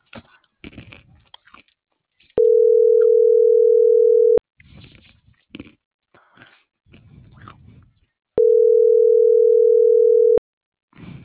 Ringback tone (North America), 440 + 480 Hz.